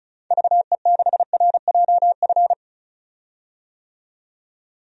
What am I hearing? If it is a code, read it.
Morse code, heard as a keyed tone, about 34 wpm, 700 Hz. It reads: VE6RJF
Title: DXCC-100 Call Signs - 6 Characters - 35wpm